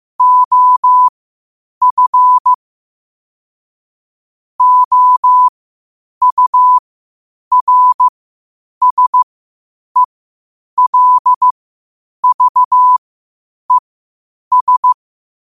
Morse code, heard as a keyed tone, about 15 wpm, spaced as for 8 wpm, 1000 Hz. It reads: OF OURSELVES
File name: qotdfiles / 8wpm